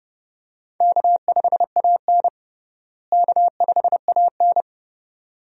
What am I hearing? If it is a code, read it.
Morse code, heard as a keyed tone, about 31 wpm, 700 Hz. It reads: K5AN K5AN
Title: USA Call Signs - 4 Characters - Repeated 2x - 30wpm